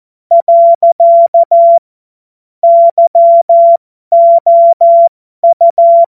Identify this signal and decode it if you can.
Morse code, heard as a keyed tone, about 14 wpm, spaced as for 12 wpm, 680 Hz. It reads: . YOU
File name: Morsekurs / Witze